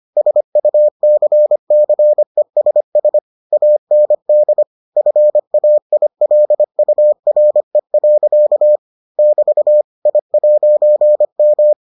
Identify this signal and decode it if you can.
Morse code, heard as a keyed tone, about 25 wpm, 600 Hz. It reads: SUCCESS AND FAILURE. <BT> I'M